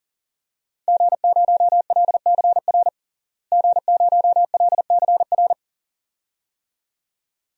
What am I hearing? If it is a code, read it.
Morse code, heard as a keyed tone, about 37 wpm, 700 Hz. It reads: G0LCR G0LCR
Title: DXCC-100 Call Signs - 5 Characters - Repeated 2x - 40wpm